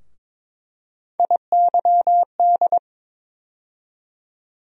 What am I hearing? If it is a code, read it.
Morse code, heard as a keyed tone, about 22 wpm, 700 Hz. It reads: IYD
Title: USA Call Sign Postfixes - 3 Letters 22wpm